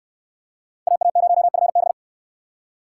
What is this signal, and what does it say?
Morse code, heard as a keyed tone, about 64 wpm, 700 Hz. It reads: AA0FB